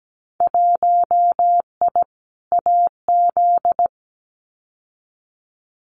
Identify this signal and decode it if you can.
Morse code, heard as a keyed tone, about 17 wpm, 700 Hz. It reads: 1I AZ
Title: Winter Field Day Exchanges 17wpm